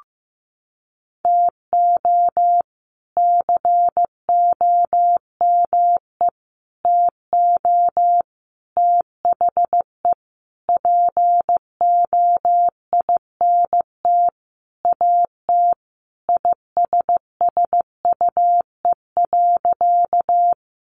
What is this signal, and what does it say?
Morse code, heard as a keyed tone, about 14 wpm, 700 Hz. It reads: TO COME TO THE POINT AT ISSUE.
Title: Sentences from Top 900 Words 15wpm